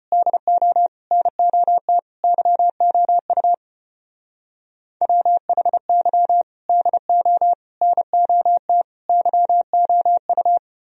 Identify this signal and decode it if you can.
Morse code, heard as a keyed tone, about 32 wpm, 700 Hz. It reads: DO NOT YOU WHY DO NOT YOU